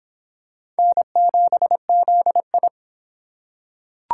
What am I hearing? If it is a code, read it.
Morse code, heard as a keyed tone, about 25 wpm, 700 Hz. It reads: N7ZI